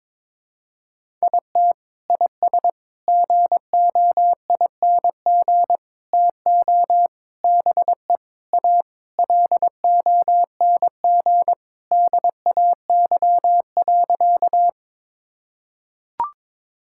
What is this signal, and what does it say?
Morse code, heard as a keyed tone, about 22 wpm, 700 Hz. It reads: IT IS GOING TO BE A LONG DAY.
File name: practice / 22wpm / Sentences with 4-Word Phrases - 22wpm